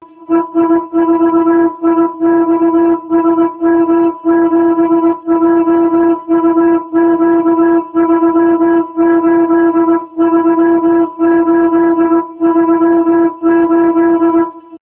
Morse code, heard as a keyed tone, about 19 wpm, 330 Hz. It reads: EI4IX SM7JUQ 383838
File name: sm7juq